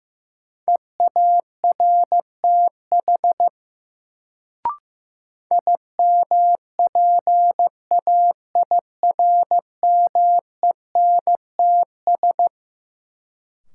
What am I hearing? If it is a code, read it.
Morse code, heard as a keyed tone, about 15 wpm, 700 Hz. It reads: EARTH IMPAIRMENTS
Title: Sets of 5 Words Encore - New Word Drill - 15wpm